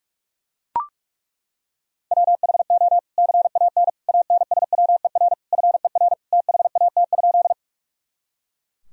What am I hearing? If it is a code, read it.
Morse code, heard as a keyed tone, about 41 wpm, 700 Hz. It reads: WHO CAN ANSWER FER THAT?
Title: Sentences from Top 300 Words and Abbreviations 45wpm